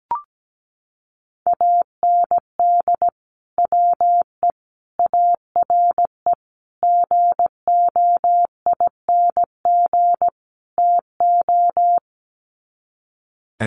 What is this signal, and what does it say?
Morse code, heard as a keyed tone, about 17 wpm, 700 Hz. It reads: AND WE ARE GOING TO
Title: Sets of 5 Words Encore2 17wpm